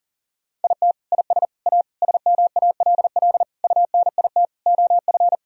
Morse code, heard as a keyed tone, about 37 wpm, 700 Hz. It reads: IT IS A SMALL UNIT OF
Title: Sentences from Top 1000 Words - Review Part 2 - 40wpm